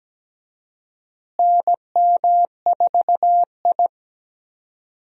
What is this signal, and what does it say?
Morse code, heard as a keyed tone, about 17 wpm, 700 Hz. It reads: NM4I
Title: USA Call Signs - 4 Characters - Repeated 3x - 17wpm